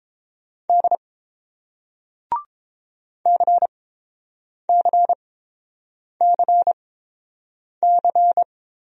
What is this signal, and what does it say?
Morse code, heard as a keyed tone, about 30 wpm, spaced as for 7 wpm, 700 Hz. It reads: D CCCC